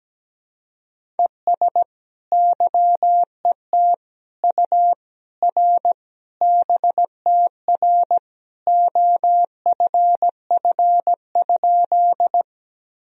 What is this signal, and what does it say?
Morse code, heard as a keyed tone, about 17 wpm, 700 Hz. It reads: ES YET U R BTR OFF?
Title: Sentences from Top 500 Words and Abbreviations 17wpm